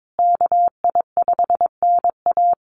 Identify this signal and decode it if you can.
Morse code, heard as a keyed tone, about 22 wpm, 700 Hz. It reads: KI5NA
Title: USA Call Signs Repeated 3x 22wpm